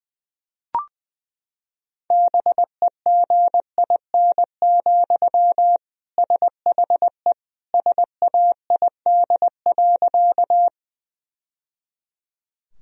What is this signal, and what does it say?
Morse code, heard as a keyed tone, about 20 wpm, 700 Hz. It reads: BEGIN, SHE SAID.
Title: Sentences from Top 1000 Words - Review Part 1 - 20wpm